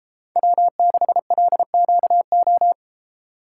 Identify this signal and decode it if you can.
Morse code, heard as a keyed tone, about 34 wpm, 700 Hz. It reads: W6LQO
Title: Speed Racing - USA Call Signs 28wpm